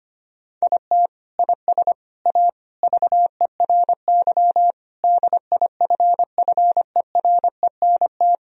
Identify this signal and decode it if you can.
Morse code, heard as a keyed tone, about 25 wpm, 700 Hz. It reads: IT IS A VERY DIFFERENT
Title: Sentences from Top 1000 Words - Review Part 4 - 25wpm